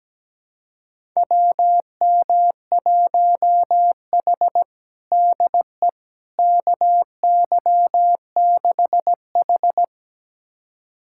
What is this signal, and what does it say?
Morse code, heard as a keyed tone, about 17 wpm, 700 Hz. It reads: WM1H DE KY6H
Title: USA Callsign DE Callsign - 4 Characters - 17wpm